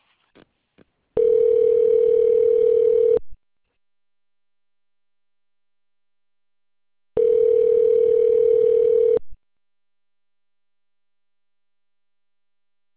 Ringback tone (North America), 440 + 480 Hz.